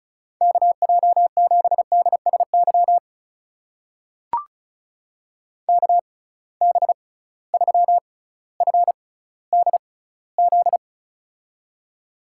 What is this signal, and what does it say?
Morse code, heard as a keyed tone, about 36 wpm, spaced as for 18 wpm, 700 Hz. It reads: KJ7DSY KB3FDZ
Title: USA Call Signs - 6 Characters - 6x Character Spacing - 35wpm